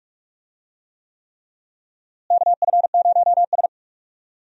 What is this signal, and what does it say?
Morse code, heard as a keyed tone, about 41 wpm, 700 Hz. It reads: KF0S
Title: USA Call Signs - 4 Characters - Repeated 3x - 45wpm